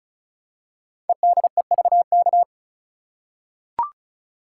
Morse code, heard as a keyed tone, about 36 wpm, 690 Hz. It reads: EDEVK
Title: Code Groups - 5 Letters - 35wpm